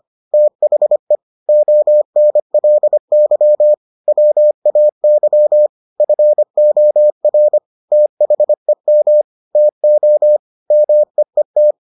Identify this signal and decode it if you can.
Morse code, heard as a keyed tone, about 25 wpm, 600 Hz. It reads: THE ONLY WAY FOR THEM TO MEET